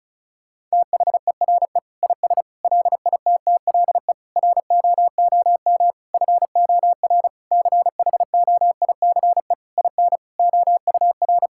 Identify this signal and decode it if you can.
Morse code, heard as a keyed tone, about 34 wpm, 700 Hz. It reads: THERE IS LITTLE ROOM FOR CHOICE IN OUR
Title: Sentences from Top 1000 Words - Review Part 6 - 35wpm